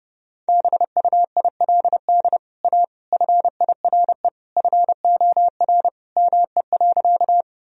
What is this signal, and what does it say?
Morse code, heard as a keyed tone, about 32 wpm, 700 Hz. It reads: BUILD A FIRE FOR ME.
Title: Sentences from Top 1000 Words - Review Part 1 - 30wpm